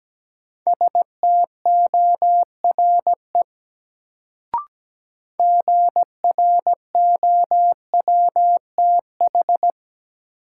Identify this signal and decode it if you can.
Morse code, heard as a keyed tone, about 17 wpm, 700 Hz. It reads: STORE GROWTH
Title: Sentences from Top 1000 Words - Review Part 5 - New Word Drill - 17wpm